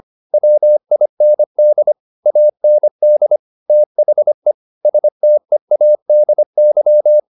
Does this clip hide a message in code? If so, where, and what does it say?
Yes: Morse code from 0.3 to 7.2 s, heard as a keyed tone, about 25 wpm, 600 Hz. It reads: WIND AND THE STEADY